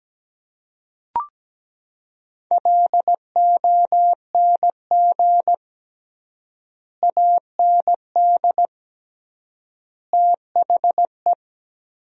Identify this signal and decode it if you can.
Morse code, heard as a keyed tone, about 17 wpm, 700 Hz. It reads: LONG AND THE
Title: 3x Word-Spacing - Sentences from Top 400 Words - 17wpm